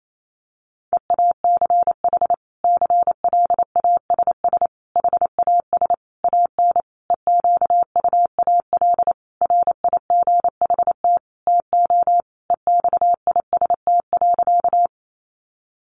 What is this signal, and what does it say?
Morse code, heard as a keyed tone, about 28 wpm, 700 Hz. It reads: EACH CLASS HAS AN EQUAL RIGHT TO EXIST.
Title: Sentences from Top 1400 Words 28wpm